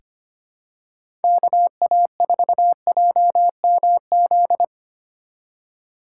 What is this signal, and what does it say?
Morse code, heard as a keyed tone, about 25 wpm, 700 Hz. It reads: KA4JMZ
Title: USA Call Signs - 6 Characters - 2x Character Spacing - 25wpm